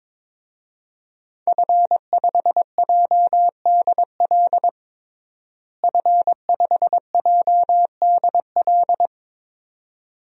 Morse code, heard as a keyed tone, about 22 wpm, 700 Hz. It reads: F5JDL F5JDL
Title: DXCC-100 Call Signs - 5 Characters - Repeated 2x - 22wpm